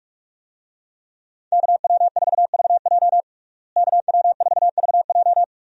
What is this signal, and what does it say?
Morse code, heard as a keyed tone, about 42 wpm, 700 Hz. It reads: KW4VJ KW4VJ
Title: USA Call Signs - 5 Characters - Repeated 3x - 45wpm